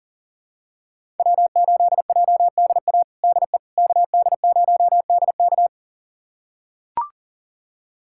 Morse code, heard as a keyed tone, about 37 wpm, 700 Hz. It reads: W8JDA DE KD0DK
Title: USA Callsign DE Callsign - 5 Characters - 40wpm